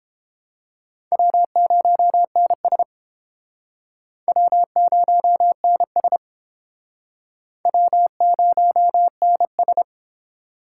Morse code, heard as a keyed tone, about 30 wpm, 700 Hz. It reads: W0NS W0NS W0NS